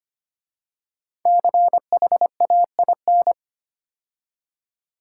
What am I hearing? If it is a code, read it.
Morse code, heard as a keyed tone, about 25 wpm, 700 Hz. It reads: CHAIN
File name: 5 Letter Words - First 3 Letters Given - 25wpm